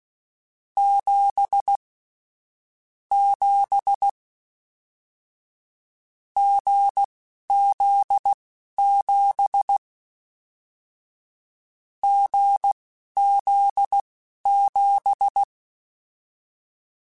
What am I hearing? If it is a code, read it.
Morse code, heard as a keyed tone, about 16 wpm, spaced as for 11 wpm, 800 Hz. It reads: 7 7 GZ7 GZ7
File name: Short Runs / Short Runs-mp3s / awj1ndb6gz7 @10